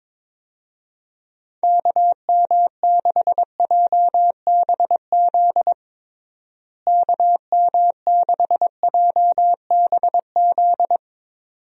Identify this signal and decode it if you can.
Morse code, heard as a keyed tone, about 21 wpm, 700 Hz. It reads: KM6JBZ KM6JBZ
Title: USA Call Signs - 6 Characters - Repeated 2x - 22wpm